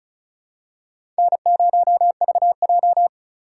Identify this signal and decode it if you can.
Morse code, heard as a keyed tone, about 36 wpm, 700 Hz. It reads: N0VJ
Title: USA Call Signs - 4 Characters - 4x Character Spacing - 35wpm